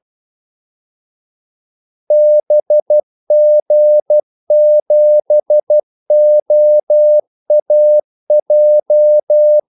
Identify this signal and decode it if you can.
Morse code, heard as a keyed tone, about 12 wpm, 600 Hz. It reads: BG7OAJ
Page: 2 ekstra ordmellemrum mellem kaldesignalerne